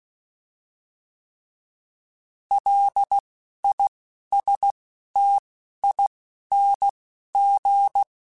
Morse code, heard as a keyed tone, about 16 wpm, 800 Hz. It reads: L I S T I N G